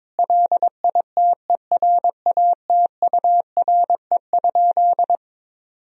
Morse code, heard as a keyed tone, about 21 wpm, 700 Hz. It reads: LITERATURE?
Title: Sentences from Top 1000 Words - Review Part 3 - 22wpm